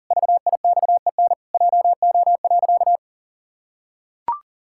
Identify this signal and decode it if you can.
Morse code, heard as a keyed tone, about 40 wpm, 700 Hz. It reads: VIXEN JO.
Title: Pangram Challenge v2 - 40wpm